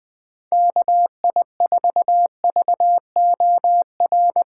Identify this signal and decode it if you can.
Morse code, heard as a keyed tone, about 20 wpm, 700 Hz. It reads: KI4VOR
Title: USA Call Signs - 4x Character Spacing - 20wpm